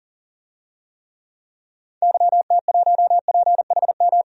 Morse code, heard as a keyed tone, about 40 wpm, 700 Hz. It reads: YT1PHM
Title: DXCC-100 Call Signs - 6 Characters - 40wpm